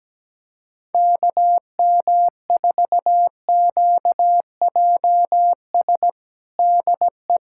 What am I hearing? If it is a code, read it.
Morse code, heard as a keyed tone, about 17 wpm, 700 Hz. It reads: KM4QJS DE
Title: USA Callsign DE Callsign - 6 Characters - 17wpm